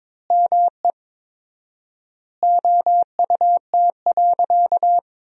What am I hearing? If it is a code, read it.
Morse code, heard as a keyed tone, about 22 wpm, 700 Hz. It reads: ME OUT.
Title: 4x Word-Spacing - Sentences from Top 300 Words - 22wpm